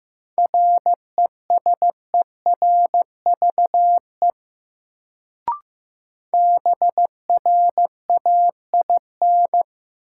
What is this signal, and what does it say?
Morse code, heard as a keyed tone, about 15 wpm, 700 Hz. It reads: RESERVE BRAIN
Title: Sentences from Top 1000 Words - Review Part 6 - New Word Drill - 15wpm